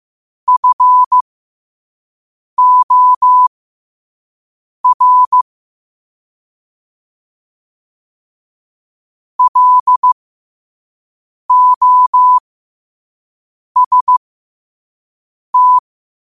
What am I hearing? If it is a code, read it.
Morse code, heard as a keyed tone, about 15 wpm, spaced as for 5 wpm, 1000 Hz. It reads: FOR LOST
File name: qotdfiles / 5wpm